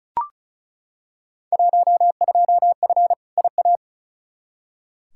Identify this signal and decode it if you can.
Morse code, heard as a keyed tone, about 36 wpm, 700 Hz. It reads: 12F IA
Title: Field Day Exchanges Encore - 35wpm